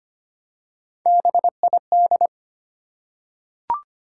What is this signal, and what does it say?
Morse code, heard as a keyed tone, about 25 wpm, 700 Hz. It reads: BID
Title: Three Letter Words - 25wpm